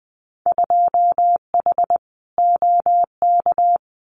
Morse code, heard as a keyed tone, about 20 wpm, 700 Hz. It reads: 2H OK
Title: Winter Field Day Exchanges Encore 20wpm